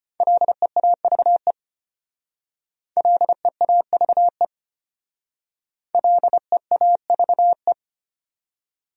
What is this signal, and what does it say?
Morse code, heard as a keyed tone, about 32 wpm, 700 Hz. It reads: LEAVE LEAVE LEAVE